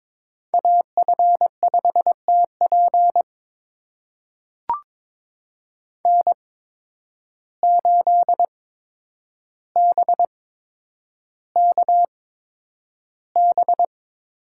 Morse code, heard as a keyed tone, about 22 wpm, spaced as for 9 wpm, 700 Hz. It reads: AF5TP N8BKB
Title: USA Call Signs - 5 Characters - 8x Character Spacing - 22wpm